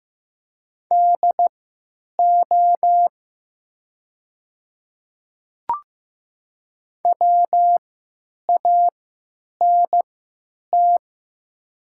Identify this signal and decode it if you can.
Morse code, heard as a keyed tone, about 15 wpm, spaced as for 9 wpm, 700 Hz. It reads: DO WANT
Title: 3x Character Spacing - Top 200 Words - 15wpm